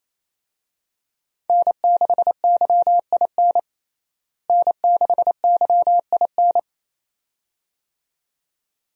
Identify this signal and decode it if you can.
Morse code, heard as a keyed tone, about 28 wpm, 700 Hz. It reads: N6YIN N6YIN
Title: USA Call Signs - 5 Characters - Repeated 2x - 28wpm